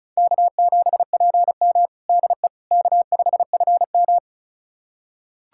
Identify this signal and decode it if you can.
Morse code, heard as a keyed tone, about 36 wpm, 700 Hz. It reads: K7PM DE K5FM
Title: USA Callsign DE Callsign - 4 Characters - 35wpm